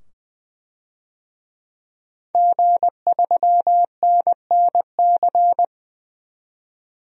Morse code, heard as a keyed tone, about 20 wpm, 700 Hz. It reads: G3NNC
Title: DXCC-100 Call Signs - 5 Characters - 20wpm